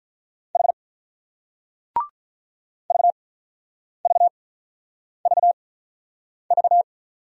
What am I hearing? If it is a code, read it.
Morse code, heard as a keyed tone, about 47 wpm, 700 Hz. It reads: H V V V V